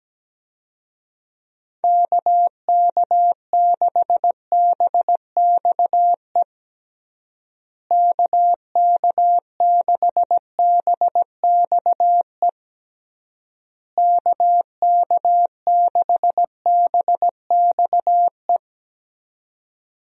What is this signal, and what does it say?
Morse code, heard as a keyed tone, about 17 wpm, 700 Hz. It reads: KK6BXE KK6BXE KK6BXE
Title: USA Call Signs - 6 Characters - Repeated 3x - 17wpm